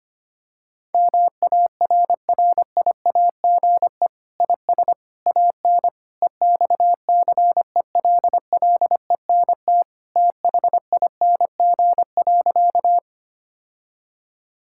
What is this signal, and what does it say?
Morse code, heard as a keyed tone, about 25 wpm, 700 Hz. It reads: MARRIAGE IS AN EXCELLENT THING.
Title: Sentences from Top 1000 Words - Review Part 5 - 25wpm